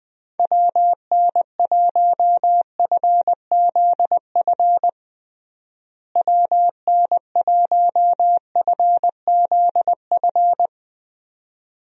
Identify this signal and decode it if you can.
Morse code, heard as a keyed tone, about 20 wpm, 700 Hz. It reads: WN1FZF WN1FZF
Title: USA Call Signs - 6 Characters - Repeated 2x - 20wpm